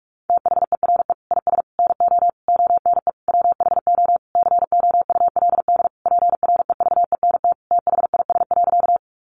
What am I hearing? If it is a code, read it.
Morse code, heard as a keyed tone, about 44 wpm, 700 Hz. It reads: THERE IS NO ONE WHO COULD PREVENT THIS.
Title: Sentences from Top 1100 Words 45wpm